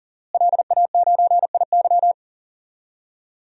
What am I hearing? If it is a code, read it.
Morse code, heard as a keyed tone, about 40 wpm, 700 Hz. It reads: LA9IY
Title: DXCC-100 Call Signs - 5 Characters - Repeated 2x - 40wpm